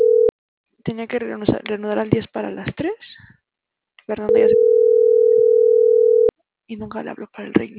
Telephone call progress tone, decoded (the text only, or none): ringback tone